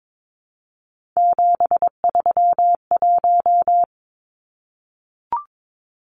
Morse code, heard as a keyed tone, about 22 wpm, 700 Hz. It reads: 731
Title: Numbers - 3-Digit 22wpm